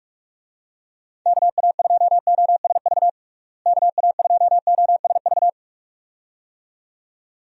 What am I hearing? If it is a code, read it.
Morse code, heard as a keyed tone, about 41 wpm, 700 Hz. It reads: KA2OSV KA2OSV
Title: USA Call Signs - 6 Characters - Repeated 2x - 45wpm